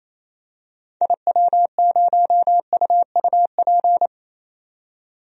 Morse code, heard as a keyed tone, about 28 wpm, 700 Hz. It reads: IW0UUP
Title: DXCC-100 Call Signs - 6 Characters - 28wpm